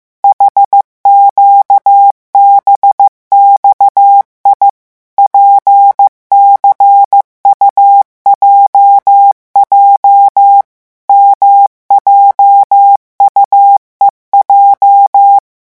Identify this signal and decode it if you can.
Morse code, heard as a keyed tone, about 15 wpm, 800 Hz. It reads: HQBXI PCUJJ MJUEJ
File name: morse mp3 / revision8-vitesse 14 mots